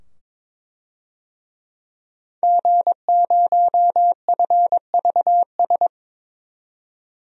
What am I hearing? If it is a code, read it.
Morse code, heard as a keyed tone, about 22 wpm, 700 Hz. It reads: G0FVS
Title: DXCC-100 Call Signs - 5 Characters - 22wpm